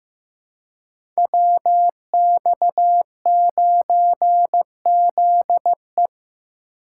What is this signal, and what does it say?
Morse code, heard as a keyed tone, about 15 wpm, 700 Hz. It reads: WX9ZE
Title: Code Groups - 5 Letter:Numbers - 15wpm